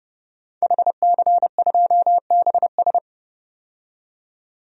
Morse code, heard as a keyed tone, about 30 wpm, 700 Hz. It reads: HC2BS
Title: DXCC-100 Call Signs - 5 Characters - 30wpm